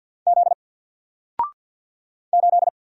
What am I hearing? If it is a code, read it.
Morse code, heard as a keyed tone, about 48 wpm, 700 Hz. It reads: Z 8
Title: Single Letter-Number Alt 50wpm